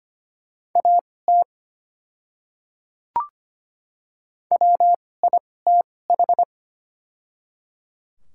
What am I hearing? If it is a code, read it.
Morse code, heard as a keyed tone, about 25 wpm, spaced as for 18 wpm, 700 Hz. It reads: AT WITH
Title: 2x Character Spacing - Top 200 Words - 25wpm